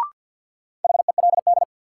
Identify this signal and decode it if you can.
Morse code, heard as a keyed tone, about 52 wpm, 700 Hz. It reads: HELD
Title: Sentences from Top 1600 Words - New Word Drill Alt - 50wpm